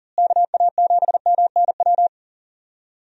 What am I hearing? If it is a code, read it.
Morse code, heard as a keyed tone, about 38 wpm, 700 Hz. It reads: KA7MNW
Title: USA Call Signs - 6 Characters - Repeated 3x - 40wpm